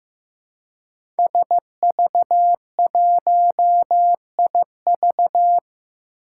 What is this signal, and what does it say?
Morse code, heard as a keyed tone, about 15 wpm, 700 Hz. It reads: SV1IV
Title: Five Challenging Letter-Number Combinations 15wpm